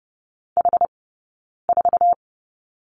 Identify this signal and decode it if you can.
Morse code, heard as a keyed tone, about 30 wpm, 700 Hz. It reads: H 4
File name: course / 30:10wpm / 086 Lesson 15d - Number 5 - Two Characters 30wpm